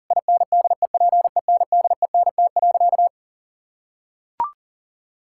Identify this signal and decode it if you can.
Morse code, heard as a keyed tone, about 40 wpm, 700 Hz. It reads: INDEPENDENT.
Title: Sentences from Top 1000 Words - Review Part 6 - 40wpm